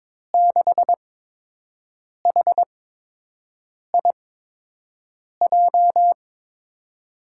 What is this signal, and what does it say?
Morse code, heard as a keyed tone, about 22 wpm, spaced as for 6 wpm, 700 Hz. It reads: 6HIJ
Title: USA Call Signs - 6 Characters - 8x Character Spacing - 22wpm